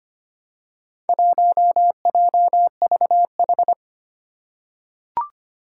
Morse code, heard as a keyed tone, about 25 wpm, 700 Hz. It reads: 1JVH
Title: Four Challenging Letter-Number Combinations 25wpm